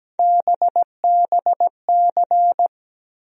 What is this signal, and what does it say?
Morse code, heard as a keyed tone, about 17 wpm, 700 Hz. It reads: BBC
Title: USA Call Sign Postfixes - 3 Letters 17wpm